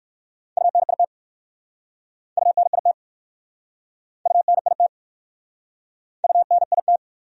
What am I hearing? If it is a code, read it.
Morse code, heard as a keyed tone, about 59 wpm, 700 Hz. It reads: UNIT UNIT UNIT UNIT